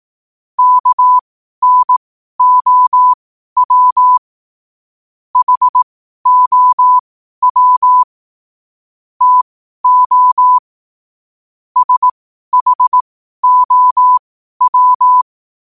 Morse code, heard as a keyed tone, about 18 wpm, spaced as for 12 wpm, 1000 Hz. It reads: KNOW HOW TO SHOW
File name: qotdfiles / 12wpm